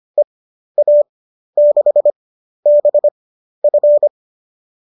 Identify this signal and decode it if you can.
Morse code, heard as a keyed tone, about 25 wpm, spaced as for 12 wpm, 590 Hz. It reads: EA6BF
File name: qso009 12